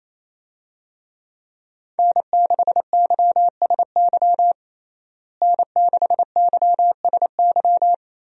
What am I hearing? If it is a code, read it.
Morse code, heard as a keyed tone, about 28 wpm, 700 Hz. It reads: N6YSY N6YSY
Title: USA Call Signs - 5 Characters - Repeated 2x - 28wpm